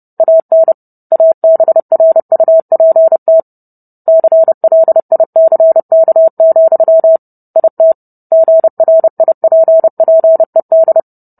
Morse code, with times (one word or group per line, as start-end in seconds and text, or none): 0.2-0.7 AN
1.1-3.4 ABRUPT
4.1-7.2 CLICK,
7.6-7.9 IT
8.3-11.0 GRIPPED